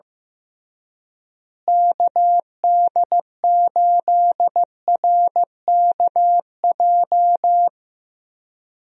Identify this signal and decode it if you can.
Morse code, heard as a keyed tone, about 15 wpm, 700 Hz. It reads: KD8RKJ
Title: USA Call Signs - 6 Characters - Repeated 3x - 15wpm